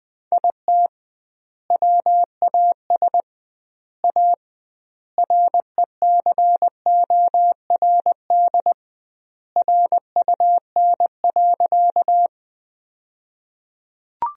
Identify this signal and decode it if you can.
Morse code, heard as a keyed tone, about 21 wpm, 700 Hz. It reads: IT WAS A RECORD RUN.
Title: 2x Word-Spacing - Sentences from Top 500 Words - 20wpm